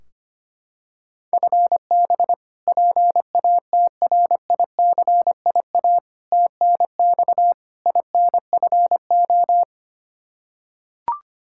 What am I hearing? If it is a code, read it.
Morse code, heard as a keyed tone, about 25 wpm, 700 Hz. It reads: FB PATRICIA TNX INFO
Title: Practice QSO - Exchange 3 - More Info - 25wpm